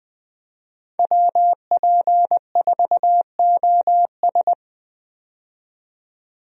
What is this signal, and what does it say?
Morse code, heard as a keyed tone, about 20 wpm, 700 Hz. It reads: WP4OS
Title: USA Call Signs - 5 Characters - 20wpm